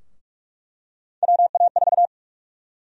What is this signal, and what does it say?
Morse code, heard as a keyed tone, about 42 wpm, 700 Hz. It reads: WA4
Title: USA Call Sign Prefixes - 45wpm